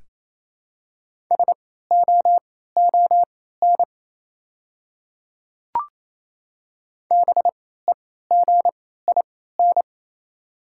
Morse code, heard as a keyed tone, about 28 wpm, spaced as for 16 wpm, 700 Hz. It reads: SOON BEGIN
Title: 3x Character Spacing - Top 500 Words - 28wpm